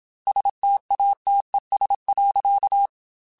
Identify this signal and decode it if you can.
Morse code, heard as a keyed tone, about 27 wpm, 800 Hz. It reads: STATES.
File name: Technical Subjects / Technical Subjects-mp3s / AND Gate @25